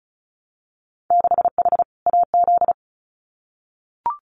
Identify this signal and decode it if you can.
Morse code, heard as a keyed tone, about 34 wpm, 700 Hz. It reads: 6H AZ
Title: Winter Field Day Exchanges Encore 35wpm